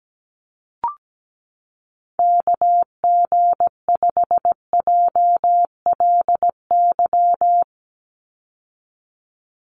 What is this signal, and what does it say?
Morse code, heard as a keyed tone, about 17 wpm, 700 Hz. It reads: KG5JLY
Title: USA Call Signs 17wpm